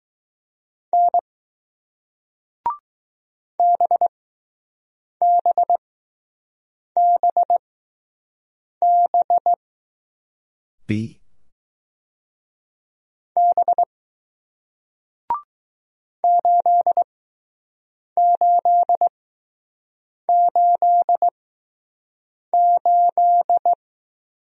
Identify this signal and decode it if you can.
Morse code, heard as a keyed tone, about 20 wpm, spaced as for 6 wpm, 700 Hz. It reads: N BBBB B 8888